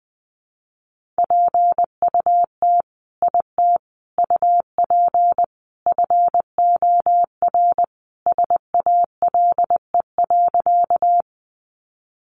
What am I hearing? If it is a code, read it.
Morse code, heard as a keyed tone, about 20 wpm, 700 Hz. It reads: PUT IT UP FOR SALE.
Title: Sentences from Top 1400 Words 20wpm